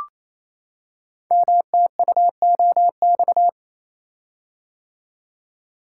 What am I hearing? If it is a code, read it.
Morse code, heard as a keyed tone, about 28 wpm, 700 Hz. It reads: MTUOX
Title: Code Groups - 5 Letters - 28wpm